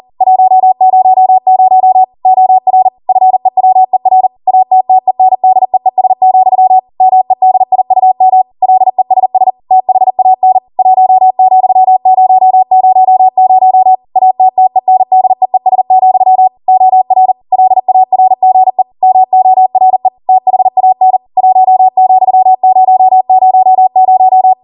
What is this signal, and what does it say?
Morse code, heard as a keyed tone, about 40 wpm, 750 Hz. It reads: 100 OR FEWER ATTENDEES, MEDIUM LESS THAN 1,000 ATTENDEES, OR LARGE MORE THAN 1,000